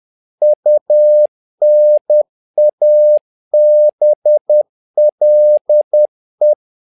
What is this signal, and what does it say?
Morse code, heard as a keyed tone, about 10 wpm, 600 Hz. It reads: UNABLE
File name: learning center / tower trouble / tower trouble 10wpm